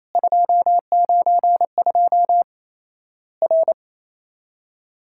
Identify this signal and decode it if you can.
Morse code, heard as a keyed tone, about 28 wpm, 700 Hz. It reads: 292 R
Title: Worked All Europe DX Contest - QTC Exchange - 28wpm